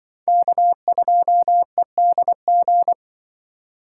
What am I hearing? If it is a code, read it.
Morse code, heard as a keyed tone, about 23 wpm, 700 Hz. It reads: K2EDG
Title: Speed Racing - USA 1x3 Call Signs 20wpm